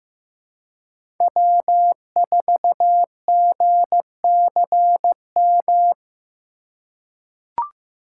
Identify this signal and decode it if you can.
Morse code, heard as a keyed tone, about 15 wpm, 700 Hz. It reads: W4GCM
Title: USA Call Signs - 2x Character Spacing - 15wpm